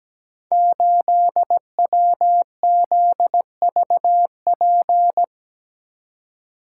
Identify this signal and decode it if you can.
Morse code, heard as a keyed tone, about 17 wpm, 700 Hz. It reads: 8WZVP
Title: Code Groups - 5 Letter:Numbers - 17wpm